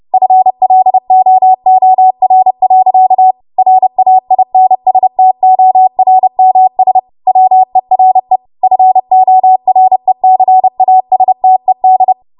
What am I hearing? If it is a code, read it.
Morse code, heard as a keyed tone, about 30 wpm, 750 Hz. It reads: FLOOR. RAINSTORMS WERE FORECASTED
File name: Morse / 30WPM